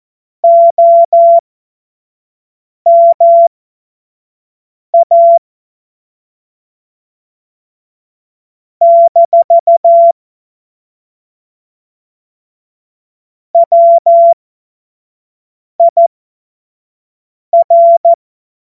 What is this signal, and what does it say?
Morse code, heard as a keyed tone, about 14 wpm, spaced as for 5 wpm, 680 Hz. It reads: OMA - WIR
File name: Morsekurs / Witze / Witz 05. Oktober 2021, 25 F70